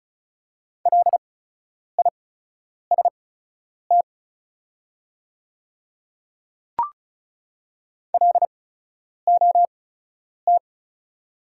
Morse code, heard as a keyed tone, about 36 wpm, spaced as for 10 wpm, 690 Hz. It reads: LIST LOT
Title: 8x Character Spacing - Top 500 Words - 35wpm